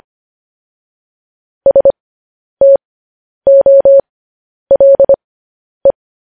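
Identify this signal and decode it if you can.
Morse code, heard as a keyed tone, about 25 wpm, spaced as for 10 wpm, 550 Hz. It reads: STOLE
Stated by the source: Morse code news headlines broadcast.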